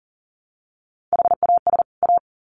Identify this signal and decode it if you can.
Morse code, heard as a keyed tone, about 40 wpm, 700 Hz. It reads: HAS A
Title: Sets of 2 Words 40wpm